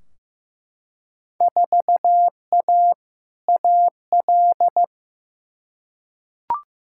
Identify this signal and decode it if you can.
Morse code, heard as a keyed tone, about 15 wpm, 700 Hz. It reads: 4A AL
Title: Field Day Exchanges - 15wpm